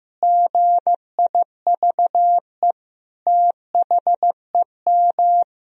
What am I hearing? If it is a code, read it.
Morse code, heard as a keyed tone, about 15 wpm, 700 Hz. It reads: GIVE THEM
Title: Sentences from Top 1000 Words - Review Part 2 - 15wpm